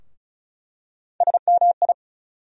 Morse code, heard as a keyed tone, about 35 wpm, 700 Hz. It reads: SMI
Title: USA Call Sign Postfixes - 3 Letters 35wpm